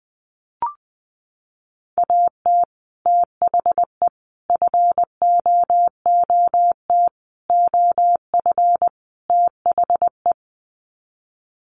Morse code, heard as a keyed tone, about 20 wpm, 700 Hz. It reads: AT THE FOOT OF THE
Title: Sets of 5 Words 20wpm